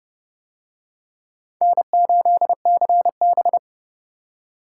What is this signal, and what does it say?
Morse code, heard as a keyed tone, about 31 wpm, 700 Hz. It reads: N8CB